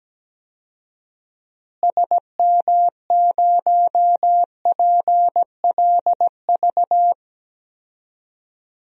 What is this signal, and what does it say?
Morse code, heard as a keyed tone, about 17 wpm, 700 Hz. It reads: SM0PLV
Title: DXCC-100 Call Signs - 6 Characters - Repeated 2x - 17wpm